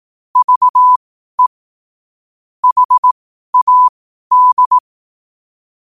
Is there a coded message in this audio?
Yes: Morse code from 0.4 to 4.8 s, heard as a keyed tone, about 18 wpm, spaced as for 12 wpm, 1000 Hz. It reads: VE HAD